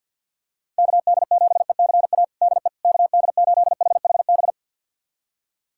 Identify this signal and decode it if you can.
Morse code, heard as a keyed tone, about 47 wpm, 700 Hz. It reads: KD7ECA DE KD8HHB
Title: USA Callsign DE Callsign - 6 Characters - 50wpm